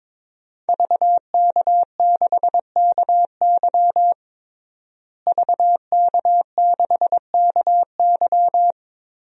Morse code, heard as a keyed tone, about 22 wpm, 700 Hz. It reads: VK6KY VK6KY
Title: DXCC-100 Call Signs - 5 Characters - Repeated 2x - 22wpm